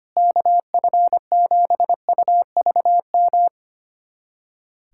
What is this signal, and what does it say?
Morse code, heard as a keyed tone, about 25 wpm, 700 Hz. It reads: KF7UVM
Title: USA Callsign DE Callsign 25wpm